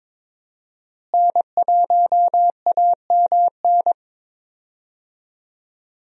Morse code, heard as a keyed tone, about 22 wpm, 700 Hz. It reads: N1AMN
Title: USA Call Signs - 5 Characters - Repeated 3x - 22wpm